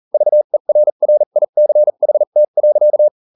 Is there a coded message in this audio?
Yes: Morse code from 0.1 to 3.1 s, heard as a keyed tone, about 40 wpm, spaced as for 35 wpm, 600 Hz. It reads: VERRICHT.